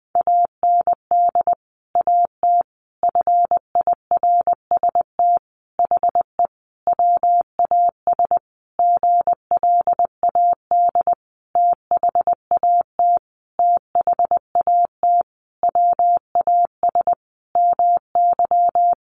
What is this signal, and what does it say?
Morse code, heard as a keyed tone, about 20 wpm, 700 Hz. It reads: AND AT FIRST HE WAS GLAD THAT THAT WAS MY
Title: Sentences from Top 1300 Words 20wpm